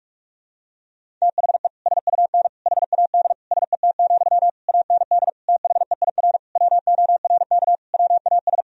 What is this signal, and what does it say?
Morse code, heard as a keyed tone, about 41 wpm, 700 Hz. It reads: THE SUN HAD SET, AND THEIR WORK WAS
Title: Sentences from Top 1000 Words - Review Part 1 - 45wpm